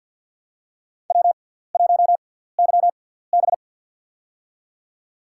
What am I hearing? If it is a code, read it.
Morse code, heard as a keyed tone, about 49 wpm, spaced as for 17 wpm, 700 Hz. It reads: W1YB
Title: USA Call Signs - 4 Characters - 6x Character Spacing - 50wpm